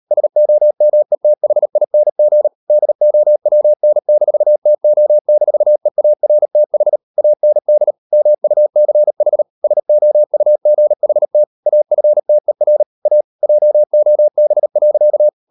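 Morse code, heard as a keyed tone, about 38 wpm, 600 Hz. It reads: SOMETHING DOWN-TO-EARTH AND MUCH SOUGHT AFTER A JOB.
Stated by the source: Morse code practice at 38 WPM, sending Ken Follett's novel Never in English, from CW med Gnister.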